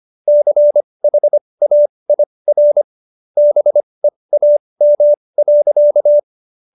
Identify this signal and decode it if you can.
Morse code, heard as a keyed tone, about 25 wpm, spaced as for 20 wpm, 590 Hz. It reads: CHAIR BEAM.